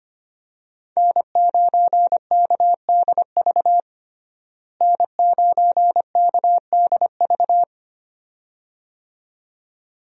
Morse code, heard as a keyed tone, about 25 wpm, 700 Hz. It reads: N9KDV N9KDV
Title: USA Call Signs - 5 Characters - Repeated 2x - 25wpm